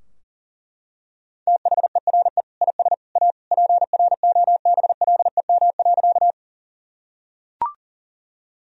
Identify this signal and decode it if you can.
Morse code, heard as a keyed tone, about 40 wpm, 700 Hz. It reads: THERE IS A PROBLEM.
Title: Sentences from Top 1000 Words - Review Part 2 - 40wpm